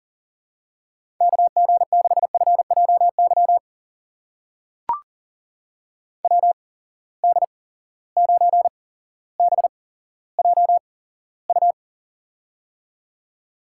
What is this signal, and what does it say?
Morse code, heard as a keyed tone, about 39 wpm, spaced as for 17 wpm, 700 Hz. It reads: KG6FJY WD9BJU
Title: USA Call Signs - 6 Characters - 8x Character Spacing - 40wpm